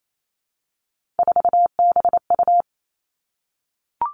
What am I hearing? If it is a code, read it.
Morse code, heard as a keyed tone, about 28 wpm, 700 Hz. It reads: 4BU
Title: Three Challenging Letter-Number Combinations 28wpm